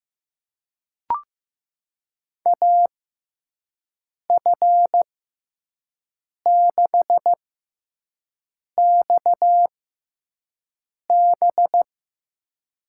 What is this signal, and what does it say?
Morse code, heard as a keyed tone, about 15 wpm, spaced as for 5 wpm, 700 Hz. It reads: AF6XB
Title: USA Call Signs - 6x Character Spacing - 15wpm